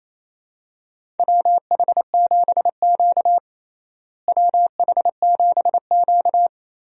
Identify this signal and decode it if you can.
Morse code, heard as a keyed tone, about 28 wpm, 700 Hz. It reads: WH7Q WH7Q
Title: USA Call Signs - 4 Characters - Repeated 3x - 28wpm